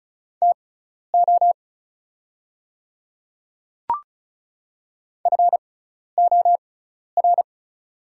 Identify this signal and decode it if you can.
Morse code, heard as a keyed tone, about 36 wpm, spaced as for 12 wpm, 700 Hz. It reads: TO FOR
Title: 6x Character Spacing - Top 200 Words - 35wpm